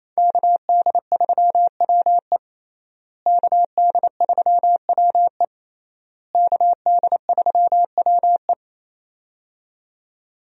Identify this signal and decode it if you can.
Morse code, heard as a keyed tone, about 28 wpm, 700 Hz. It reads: KD3WE KD3WE KD3WE
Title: USA Call Signs - 5 Characters - Repeated 3x - 28wpm